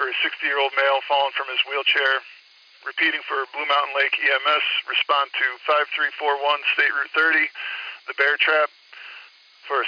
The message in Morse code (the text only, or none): none